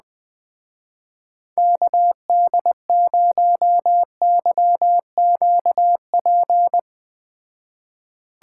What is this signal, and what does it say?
Morse code, heard as a keyed tone, about 21 wpm, 700 Hz. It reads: KD0YQP